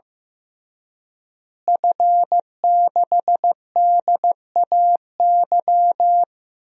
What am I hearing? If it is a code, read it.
Morse code, heard as a keyed tone, about 15 wpm, 700 Hz. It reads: F6DAY
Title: DXCC-100 Call Signs - 5 Characters - Repeated 3x - 15wpm